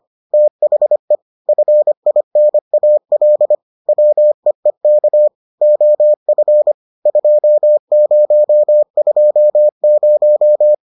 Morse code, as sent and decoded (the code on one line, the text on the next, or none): - .... / ..-. .. -. .- .-.. / .-- . -.- / --- ..-. / ..--- ----- ..--- -----
THE FINAL WEEK OF 2020